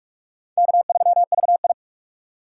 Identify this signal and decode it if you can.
Morse code, heard as a keyed tone, about 46 wpm, 700 Hz. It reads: K3VI
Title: USA Call Signs - 4 Characters - 2x Character Spacing - 45wpm